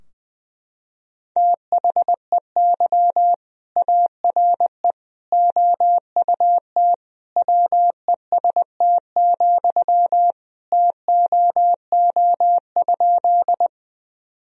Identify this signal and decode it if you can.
Morse code, heard as a keyed tone, about 20 wpm, 700 Hz. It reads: THEY ARE OUT WEST, TOO?
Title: Sentences from Top 1000 Words - Review Part 2 - 20wpm